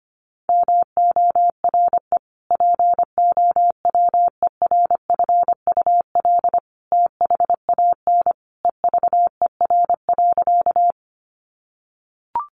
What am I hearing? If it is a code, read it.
Morse code, heard as a keyed tone, about 25 wpm, 700 Hz. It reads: MORE POWERFUL THAN EVER.
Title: Sentences from Top 1300 Words 25wpm